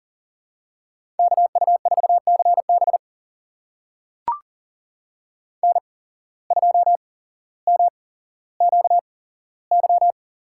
Morse code, heard as a keyed tone, about 41 wpm, spaced as for 17 wpm, 700 Hz. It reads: KU4CB N2MQY
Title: USA Call Signs - 5 Characters - 8x Character Spacing - 40wpm